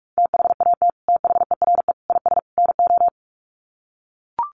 Morse code, heard as a keyed tone, about 44 wpm, 700 Hz. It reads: THAT THERE IS NO
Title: Sets of 4 Words 45wpm